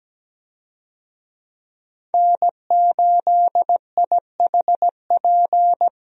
Morse code, heard as a keyed tone, about 17 wpm, 700 Hz. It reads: N8IHP
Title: USA Call Signs - 5 Characters - Repeated 3x - 17wpm